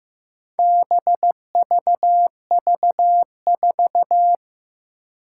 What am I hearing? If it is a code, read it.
Morse code, heard as a keyed tone, about 15 wpm, 700 Hz. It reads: BVV4
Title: Four Challenging Letter-Number Combinations 15wpm